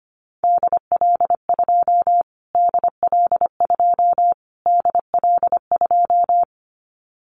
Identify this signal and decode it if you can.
Morse code, heard as a keyed tone, about 25 wpm, 700 Hz. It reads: DL2 DL2 DL2
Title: Callsign Prefixes for Top 25 DXCC - Repeated 3x - 25wpm